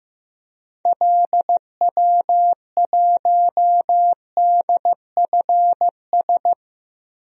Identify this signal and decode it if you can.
Morse code, heard as a keyed tone, about 15 wpm, 700 Hz. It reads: LW1DFS
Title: DXCC-100 Call Signs - 6 Characters - Repeated 2x - 15wpm